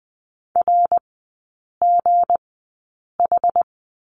Morse code, heard as a keyed tone, about 20 wpm, spaced as for 8 wpm, 700 Hz. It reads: RGH
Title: 172 Lesson 28e - Number 9 - Three Characters 20wpm